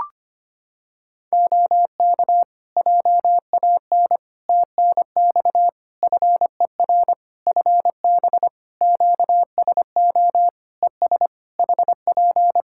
Morse code, heard as a keyed tone, about 25 wpm, 700 Hz. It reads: OK JAN TNX FER FB QSO ES HP
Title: K1USN SST QSO Practice 25wpm